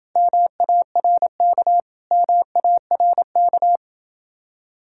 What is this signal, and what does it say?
Morse code, heard as a keyed tone, about 29 wpm, 700 Hz. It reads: MARK MARK